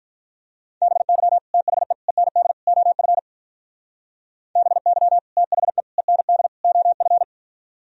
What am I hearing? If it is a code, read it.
Morse code, heard as a keyed tone, about 44 wpm, 700 Hz. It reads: BY THE END OF BY THE END OF